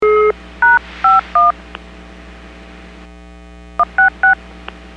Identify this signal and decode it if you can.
DTMF, heard as telephone touch tones, tones of about 150 ms. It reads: #51 166
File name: WPS / 02388 06-25 WPS 451.25 WPS